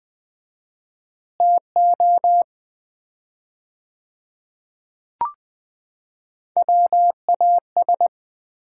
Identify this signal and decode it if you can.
Morse code, heard as a keyed tone, about 20 wpm, 700 Hz. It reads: TO WAS
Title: 4x Word-Spacing - Sets of 3 Words - 20wpm